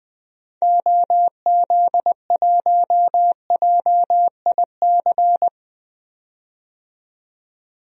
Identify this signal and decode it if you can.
Morse code, heard as a keyed tone, about 20 wpm, 700 Hz. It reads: OZ1JIC
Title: DXCC-100 Call Signs - 6 Characters - 20wpm